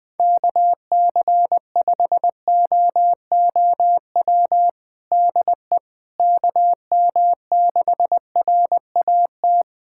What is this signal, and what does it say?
Morse code, heard as a keyed tone, about 21 wpm, 700 Hz. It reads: KC5OOW DE KM6RAT
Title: USA Callsign DE Callsign - 6 Characters - 20wpm